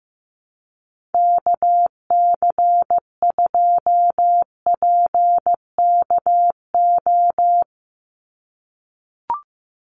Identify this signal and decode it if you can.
Morse code, heard as a keyed tone, about 15 wpm, 700 Hz. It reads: KC2PKO
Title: USA Call Signs 15wpm